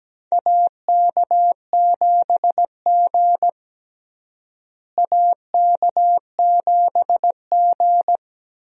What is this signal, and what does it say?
Morse code, heard as a keyed tone, about 17 wpm, 700 Hz. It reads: AK7G AK7G
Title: USA Call Signs - 4 Characters - Repeated 2x - 17wpm